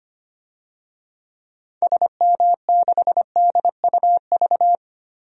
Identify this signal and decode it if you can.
Morse code, heard as a keyed tone, about 25 wpm, 700 Hz. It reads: SM6DUV
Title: DXCC-100 Call Signs - 6 Characters - 25wpm